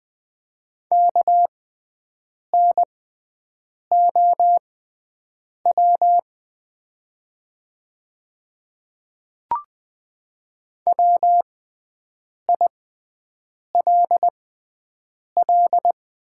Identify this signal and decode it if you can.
Morse code, heard as a keyed tone, about 20 wpm, spaced as for 7 wpm, 700 Hz. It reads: KNOW WILL
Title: 6x Character Spacing - Top 100 Words - 20wpm